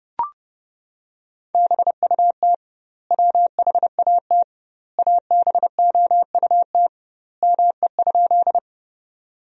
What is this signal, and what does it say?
Morse code, heard as a keyed tone, about 30 wpm, 700 Hz. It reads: BUT WHAT ABOUT ME?
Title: 2x Word-Spacing - Sentences from Top 200 Words - 30wpm